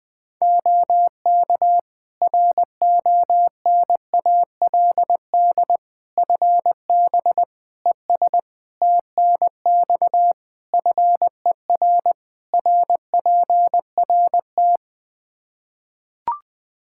Morse code, heard as a keyed tone, about 20 wpm, 700 Hz. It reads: OK RONALD FB ES TNX FER RPRT
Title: Practice QSO - Exchange 2 - Info - 20wpm